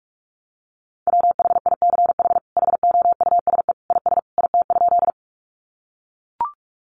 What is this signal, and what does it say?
Morse code, heard as a keyed tone, about 44 wpm, 700 Hz. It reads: WHICH HOUSE IS IT?
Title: Sentences from Top 200 Words 45wpm